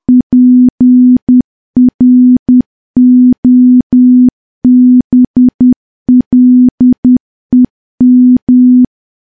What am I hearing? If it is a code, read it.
Morse code, heard as a keyed tone, about 10 wpm, 250 Hz. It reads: PROBLEM
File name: morse / morse-training-10WPM